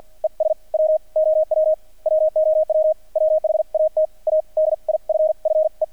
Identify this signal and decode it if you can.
Morse code, heard as a keyed tone, about 48 wpm, spaced as for 40 wpm, 620 Hz. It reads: ES WOW WOW WHAT A DIFFE
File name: archived / CRSnet